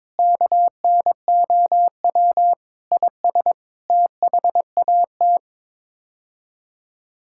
Morse code, heard as a keyed tone, about 22 wpm, 700 Hz. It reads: KNOW IS THAT
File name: practice / 22wpm / Sets of 6 Words - 22wpm